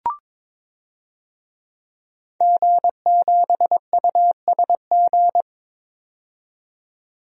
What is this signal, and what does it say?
Morse code, heard as a keyed tone, about 22 wpm, 700 Hz. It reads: G7USG
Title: DXCC-100 Call Signs - 5 Characters - 22wpm